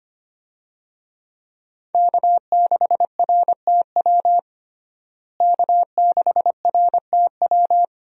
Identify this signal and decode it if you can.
Morse code, heard as a keyed tone, about 25 wpm, 700 Hz. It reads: K6RTW K6RTW
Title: USA Call Signs - 5 Characters - Repeated 2x - 25wpm